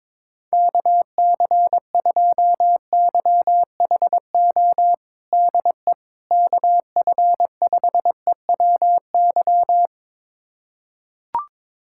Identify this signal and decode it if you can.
Morse code, heard as a keyed tone, about 22 wpm, 700 Hz. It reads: KC2YHO DE KF5EWY
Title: USA Callsign DE Callsign - 6 Characters - 22wpm